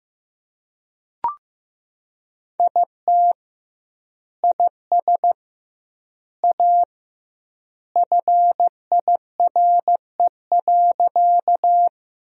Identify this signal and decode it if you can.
Morse code, heard as a keyed tone, about 15 wpm, 700 Hz. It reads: IT IS A FIRE.
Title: 2x Word-Spacing - Sentences from Top 400 Words - 15wpm